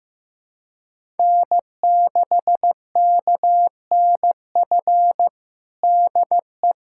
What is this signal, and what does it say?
Morse code, heard as a keyed tone, about 15 wpm, 700 Hz. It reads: N6KNF DE
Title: USA Callsign DE Callsign - 5 Characters - 15wpm